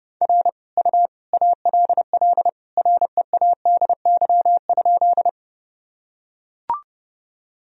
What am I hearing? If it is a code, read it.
Morse code, heard as a keyed tone, about 29 wpm, 700 Hz. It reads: R U ALL READY?
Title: Sentences from Top 400 Words and Abbreviations 30wpm